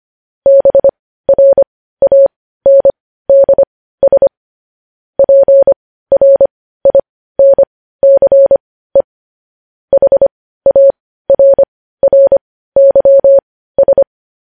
Morse code, heard as a keyed tone, about 25 wpm, spaced as for 15 wpm, 550 Hz. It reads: BRANDS PRINCE HARRYS
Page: News Headlines in Morse Code at 15 WPM